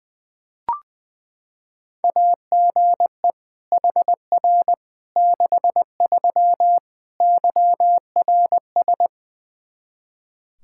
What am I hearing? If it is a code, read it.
Morse code, heard as a keyed tone, about 21 wpm, 700 Hz. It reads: AGE HR 63 YRS
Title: K1USN SST QSO Practice 20wpm